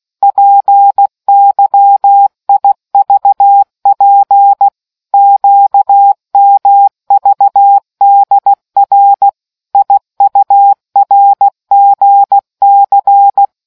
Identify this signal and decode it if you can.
Morse code, heard as a keyed tone, about 16 wpm, 800 Hz. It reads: PYIVP QMVDR IURGC